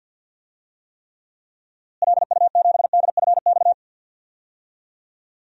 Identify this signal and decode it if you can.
Morse code, heard as a keyed tone, about 53 wpm, 700 Hz. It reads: LU7DFX
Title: DXCC-100 Call Signs - 6 Characters - Repeated 2x - 50wpm